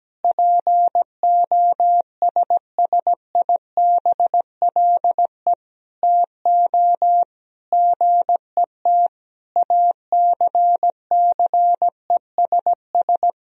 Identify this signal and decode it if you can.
Morse code, heard as a keyed tone, about 17 wpm, 700 Hz. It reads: POSSIBLE TO GET ACCESS
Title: Sentences from Top 1000 Words - Review Part 4 - 17wpm